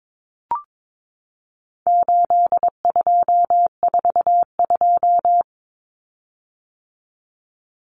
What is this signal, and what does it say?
Morse code, heard as a keyed tone, about 22 wpm, 700 Hz. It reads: 8242
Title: Common Cut Numbers - 4-Digit 22wpm